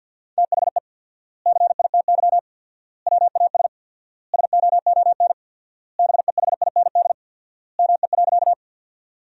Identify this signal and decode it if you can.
Morse code, heard as a keyed tone, about 52 wpm, 700 Hz. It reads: THE CITY WAS SOON BEHIND ME.
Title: 4x Word-Spacing - Sentences from Top 500 Words - 50wpm